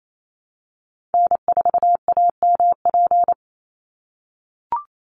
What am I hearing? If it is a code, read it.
Morse code, heard as a keyed tone, about 28 wpm, 700 Hz. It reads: N4AMP
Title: USA Call Signs 28wpm